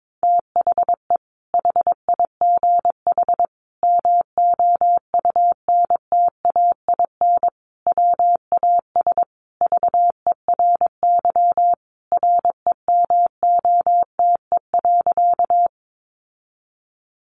Morse code, heard as a keyed tone, about 22 wpm, 700 Hz. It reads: THE HIGH MOUNTAIN WAS VERY REMOTE.
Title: Sentences from Top 1300 Words 22wpm